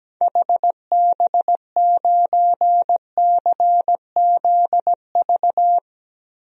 Morse code, heard as a keyed tone, about 17 wpm, 700 Hz. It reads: HB9CZV
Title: DXCC-100 Call Signs - 6 Characters - Repeated 3x - 17wpm